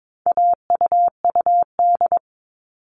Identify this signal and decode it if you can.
Morse code, heard as a keyed tone, about 22 wpm, 700 Hz. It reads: AUUD
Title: All Cut Numbers - 4-Digit 22wpm